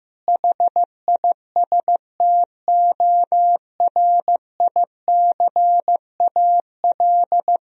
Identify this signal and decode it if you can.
Morse code, heard as a keyed tone, about 15 wpm, 700 Hz. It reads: HISTORICAL
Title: Sentences from Top 1000 Words - Review Part 4 - 15wpm